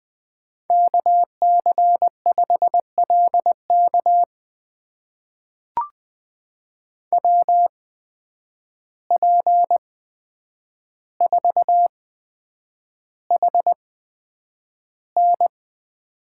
Morse code, heard as a keyed tone, about 20 wpm, spaced as for 9 wpm, 700 Hz. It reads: KC5LK WP4HN
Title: USA Call Signs - 5 Characters - 8x Character Spacing - 20wpm